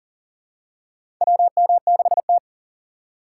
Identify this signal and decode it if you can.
Morse code, heard as a keyed tone, about 40 wpm, 700 Hz. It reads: WM6T
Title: USA Call Signs - 4 Characters - 3x Character Spacing - 40wpm